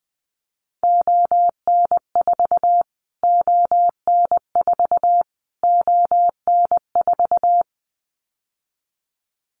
Morse code, heard as a keyed tone, about 20 wpm, 700 Hz. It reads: ON4 ON4 ON4
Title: Callsign Prefixes for Top 25 DXCC - Repeated 3x - 20wpm